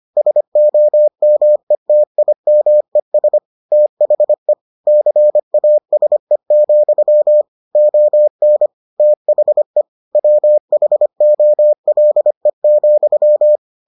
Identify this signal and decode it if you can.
Morse code, heard as a keyed tone, about 25 wpm, 600 Hz. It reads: SOMETIMES THE CASE, ON THE WHOLE,